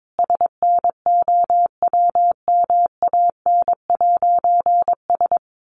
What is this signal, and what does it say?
Morse code, heard as a keyed tone, about 22 wpm, 700 Hz. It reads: SNOWMAN'S
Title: Easy Phrases 22wpm